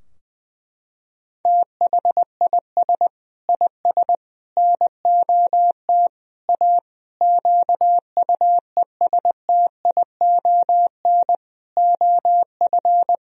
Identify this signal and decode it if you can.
Morse code, heard as a keyed tone, about 21 wpm, 700 Hz. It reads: THIS IS NOT A QUESTION OF
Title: Sentences from Top 1000 Words - Review Part 2 - 20wpm